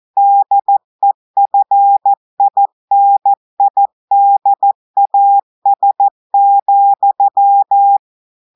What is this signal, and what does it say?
Morse code, heard as a keyed tone, about 14 wpm, 800 Hz. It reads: DEFINIDAS,
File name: audios / portugais / portugais texte en clair 2 vitesse 14 mots minutes